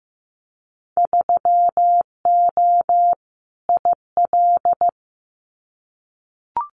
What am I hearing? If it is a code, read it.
Morse code, heard as a keyed tone, about 15 wpm, 700 Hz. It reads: 3O IL
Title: Winter Field Day Exchanges 15wpm